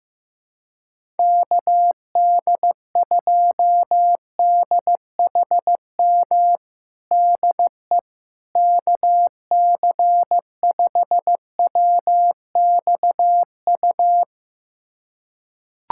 Morse code, heard as a keyed tone, about 15 wpm, 700 Hz. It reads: KD2DHM DE KC5WXU
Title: USA Callsign DE Callsign - 6 Characters - 15wpm